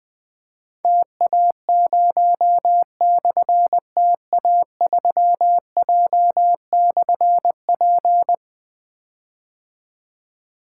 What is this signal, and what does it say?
Morse code, heard as a keyed tone, about 20 wpm, 700 Hz. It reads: TA0/TA3J/P
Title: Difficult Call Signs 20wpm